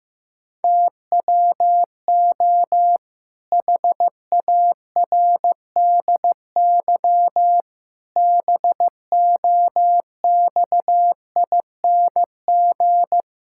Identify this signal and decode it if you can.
Morse code, heard as a keyed tone, about 15 wpm, 700 Hz. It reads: TWO HARDY BOXING
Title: Pangram Challenge v2 - 15wpm